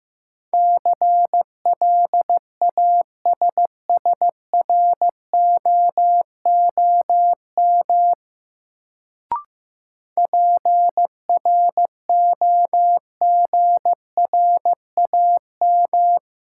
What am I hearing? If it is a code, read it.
Morse code, heard as a keyed tone, about 16 wpm, 700 Hz. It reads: CLASSROOM PROGRAM
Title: Sentences from Top 1500 Words - New Word Drill Alt - 15wpm